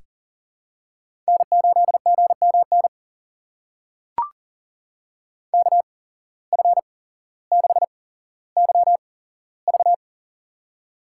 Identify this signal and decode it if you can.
Morse code, heard as a keyed tone, about 40 wpm, spaced as for 17 wpm, 700 Hz. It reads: N8GMN KF6YV
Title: USA Call Signs - 5 Characters - 8x Character Spacing - 40wpm